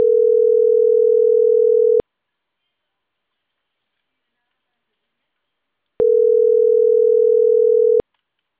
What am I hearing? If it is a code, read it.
Ringback tone (North America), 440 + 480 Hz.